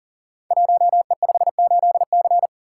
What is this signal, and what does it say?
Morse code, heard as a keyed tone, about 41 wpm, 690 Hz. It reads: 1E58C
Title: Code Groups - 5 Letter:Numbers - 40wpm